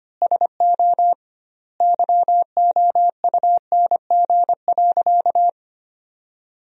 Morse code, heard as a keyed tone, about 25 wpm, 700 Hz. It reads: SO YOUNG.
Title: 2x Word-Spacing - Sentences from Top 400 Words - 25wpm